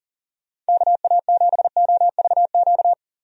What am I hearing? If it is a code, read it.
Morse code, heard as a keyed tone, about 40 wpm, 700 Hz. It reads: KA7OVQ
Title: USA Call Signs - 6 Characters - 40wpm